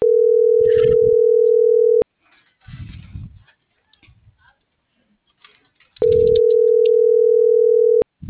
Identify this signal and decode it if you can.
Ringback tone (North America), 440 + 480 Hz.